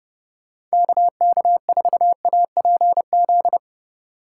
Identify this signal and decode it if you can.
Morse code, heard as a keyed tone, about 36 wpm, 700 Hz. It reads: KK4APZ
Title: Speed Racing - USA 2x3 Call Signs 30wpm